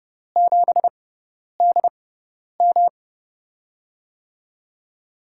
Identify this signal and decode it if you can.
Morse code, heard as a keyed tone, about 31 wpm, spaced as for 10 wpm, 700 Hz. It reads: 7DM
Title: USA Call Signs - 4 Characters - 6x Character Spacing - 30wpm